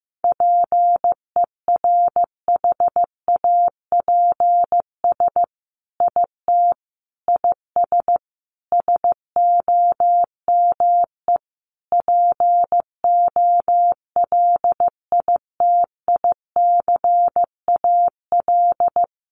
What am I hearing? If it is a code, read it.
Morse code, heard as a keyed tone, about 15 wpm, 700 Hz. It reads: PERHAPS IT IS SOME POLITICAL
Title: Sentences from Top 800 Words 15wpm